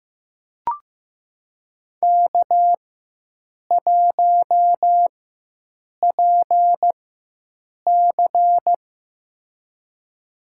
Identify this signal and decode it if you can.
Morse code, heard as a keyed tone, about 15 wpm, spaced as for 7 wpm, 700 Hz. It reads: K1PC
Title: USA Call Signs - 4 Characters - 4x Character Spacing - 15wpm